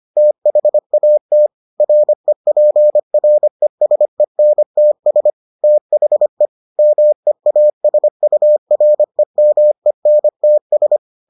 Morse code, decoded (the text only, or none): THAT REPRESENTS THE MEASUREMENTS